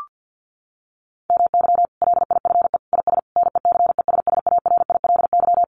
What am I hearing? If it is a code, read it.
Morse code, heard as a keyed tone, about 50 wpm, 700 Hz. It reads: MY LIFE IS NECESSARILY
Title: Sentences from Top 1400 Words 50wpm